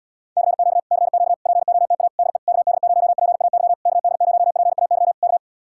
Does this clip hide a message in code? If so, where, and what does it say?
Yes: Morse code from 0.4 to 5.4 s, heard as a keyed tone, about 79 wpm, 700 Hz. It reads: CQ CQ POTA DE KD0PAY KD0PAY K